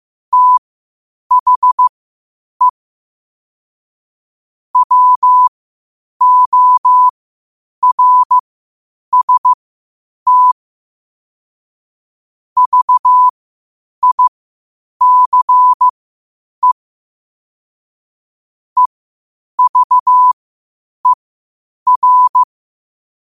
Morse code, heard as a keyed tone, about 15 wpm, spaced as for 8 wpm, 1000 Hz. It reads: THE WORST VICE EVER
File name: qotdfiles / 8wpm